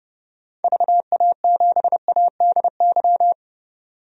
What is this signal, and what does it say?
Morse code, heard as a keyed tone, about 30 wpm, 700 Hz. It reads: VA7ADY
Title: DXCC-100 Call Signs - 6 Characters - 30wpm